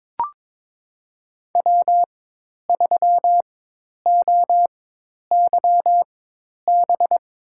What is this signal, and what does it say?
Morse code, heard as a keyed tone, about 22 wpm, spaced as for 10 wpm, 700 Hz. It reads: W3OYB
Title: USA Call Signs - 5 Characters - 4x Character Spacing - 22wpm